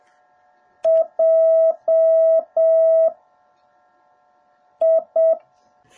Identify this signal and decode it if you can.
Morse code, heard as a keyed tone, about 7 wpm, spaced as for 4 wpm, 660 Hz. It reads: JI